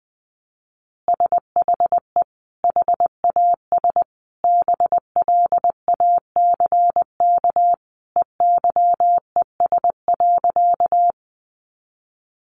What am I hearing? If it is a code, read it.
Morse code, heard as a keyed tone, about 20 wpm, 700 Hz. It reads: SHE HAS BLACK EYES.
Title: Easy Phrases 20wpm